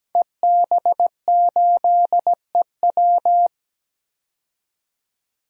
Morse code, heard as a keyed tone, about 17 wpm, 700 Hz. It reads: EB8EW
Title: DXCC-100 Call Signs - 5 Characters - 17wpm